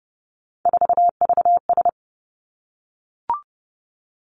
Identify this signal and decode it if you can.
Morse code, heard as a keyed tone, about 32 wpm, 700 Hz. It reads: 4VS
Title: Three Challenging Letter-Number Combinations 30wpm